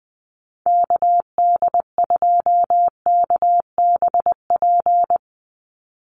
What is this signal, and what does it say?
Morse code, heard as a keyed tone, about 20 wpm, 700 Hz. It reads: KD2KBP
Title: USA Call Signs 20wpm